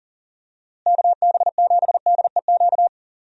Morse code, heard as a keyed tone, about 37 wpm, 700 Hz. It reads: KB7DEQ
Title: USA Call Signs - 6 Characters - Repeated 2x - 40wpm